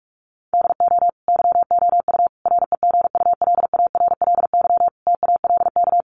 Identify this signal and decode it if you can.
Morse code, heard as a keyed tone, about 44 wpm, 700 Hz. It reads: DO YOU REGULARLY TALK
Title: Sentences from Top 1400 Words 45wpm